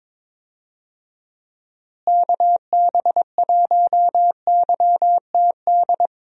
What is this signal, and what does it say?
Morse code, heard as a keyed tone, about 22 wpm, 700 Hz. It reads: KB1YTD
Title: USA Call Signs - 6 Characters - Repeated 2x - 22wpm